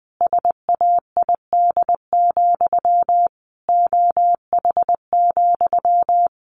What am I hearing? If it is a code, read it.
Morse code, heard as a keyed tone, about 20 wpm, 700 Hz. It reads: SAID, OH,
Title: Sentences from Top 1200 Words 20wpm